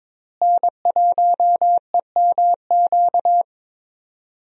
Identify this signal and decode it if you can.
Morse code, heard as a keyed tone, about 22 wpm, 700 Hz. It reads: N1EMQ
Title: USA Call Signs - 5 Characters - Repeated 2x - 22wpm